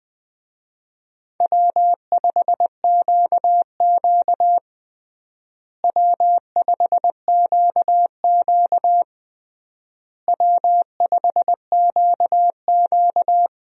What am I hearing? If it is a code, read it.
Morse code, heard as a keyed tone, about 21 wpm, 700 Hz. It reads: W5QQ W5QQ W5QQ
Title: USA Call Signs - 4 Characters - Repeated 3x - 20wpm